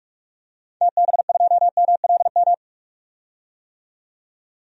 Morse code, heard as a keyed tone, about 46 wpm, 700 Hz. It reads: TB2MLM
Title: DXCC-100 Call Signs - 6 Characters - 45wpm